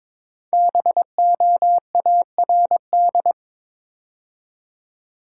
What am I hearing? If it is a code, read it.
Morse code, heard as a keyed tone, about 22 wpm, 690 Hz. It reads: BOARD
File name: 5 Letter Words - First 3 Letters Given - 22wpm